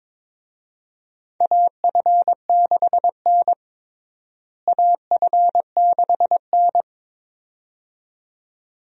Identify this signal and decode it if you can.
Morse code, heard as a keyed tone, about 23 wpm, 700 Hz. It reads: AF6N AF6N
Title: USA Call Signs - 4 Characters - Repeated 2x - 22wpm